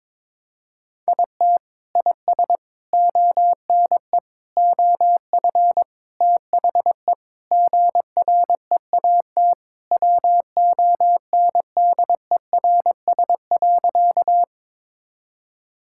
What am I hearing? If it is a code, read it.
Morse code, heard as a keyed tone, about 22 wpm, 700 Hz. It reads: IT IS ONE OF THE GREAT WONDERS.
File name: practice / 22wpm / Sentences with 4-Word Phrases - 22wpm